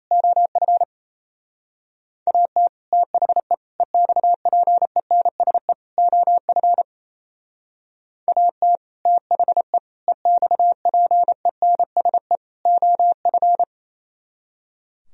Morse code, heard as a keyed tone, about 36 wpm, 700 Hz. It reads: OF AT THE EXPENSE OF AT THE EXPENSE OF